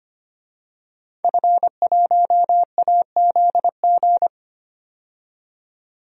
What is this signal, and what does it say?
Morse code, heard as a keyed tone, about 25 wpm, 700 Hz. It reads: F1AZG
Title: DXCC-100 Call Signs - 5 Characters - 25wpm